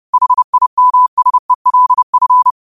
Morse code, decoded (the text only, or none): HIMSELF